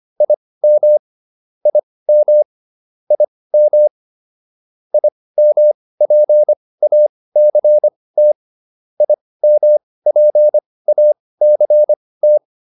Morse code, heard as a keyed tone, about 25 wpm, spaced as for 18 wpm, 600 Hz. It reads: IM IM IM IMPACT IMPACT